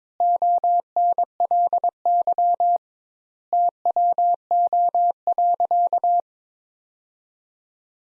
Morse code, heard as a keyed tone, about 22 wpm, 700 Hz. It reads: ONLY TWO.
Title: Sentences Top 200 Words - 22wpm